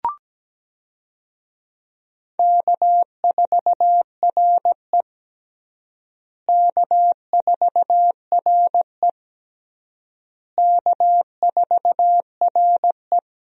Morse code, heard as a keyed tone, about 17 wpm, 700 Hz. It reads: K4RE K4RE K4RE
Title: USA Call Signs - 4 Characters - Repeated 3x - 17wpm